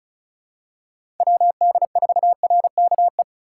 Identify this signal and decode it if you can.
Morse code, heard as a keyed tone, about 34 wpm, 700 Hz. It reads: WD4RKE
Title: USA Call Signs - 6 Characters - 3x Character Spacing - 35wpm